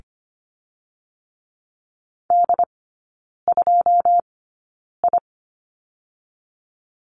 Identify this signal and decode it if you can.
Morse code, heard as a keyed tone, about 25 wpm, spaced as for 9 wpm, 690 Hz. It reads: D2I
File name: Lesson 14e - Number 2 - Three Characters 25wpm